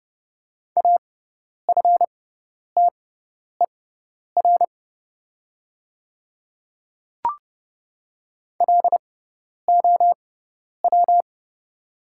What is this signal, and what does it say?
Morse code, heard as a keyed tone, about 30 wpm, spaced as for 10 wpm, 690 Hz. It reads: AFTER LOW
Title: 6x Character Spacing - Top 200 Words - 30wpm